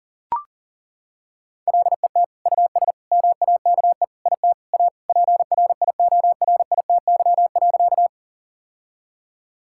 Morse code, heard as a keyed tone, about 37 wpm, 700 Hz. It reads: LET US MAKE IT A PRIORITY.
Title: Sentences from Top 1600 Words 40wpm